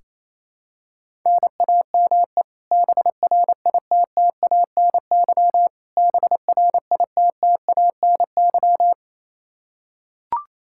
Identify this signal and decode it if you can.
Morse code, heard as a keyed tone, about 28 wpm, 700 Hz. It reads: NAME BRITTANY BRITTANY
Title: Practice QSO - Exchange 1 - Beginning - 28wpm